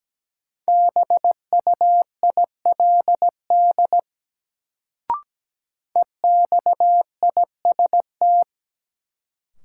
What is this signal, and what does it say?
Morse code, heard as a keyed tone, about 17 wpm, 700 Hz. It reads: BUILD EXIST
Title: Sentences from Top 1100 Words - New Word Drill Alt - 17wpm